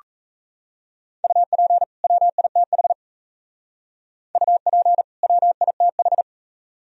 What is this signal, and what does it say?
Morse code, heard as a keyed tone, about 36 wpm, 700 Hz. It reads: UP WITH UP WITH